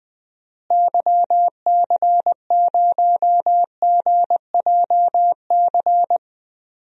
Morse code, heard as a keyed tone, about 20 wpm, 700 Hz. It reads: YC0GJC
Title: DXCC-100 Call Signs - 6 Characters - Repeated 3x - 20wpm